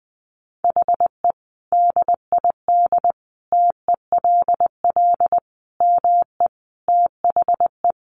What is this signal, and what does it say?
Morse code, heard as a keyed tone, about 20 wpm, 700 Hz. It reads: HE DID TELL ME THE
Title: Sentences from Top 1000 Words 20wpm